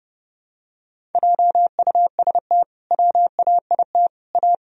Morse code, heard as a keyed tone, about 32 wpm, 700 Hz. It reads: JUST WAIT A
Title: Sentences from Top 1000 Words - Review Part 1 - 30wpm